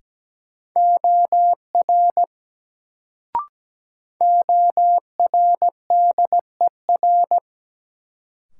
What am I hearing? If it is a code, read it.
Morse code, heard as a keyed tone, about 17 wpm, 700 Hz. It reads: OR ORDER
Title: Sets of 2 Words - New Word Drill - 17wpm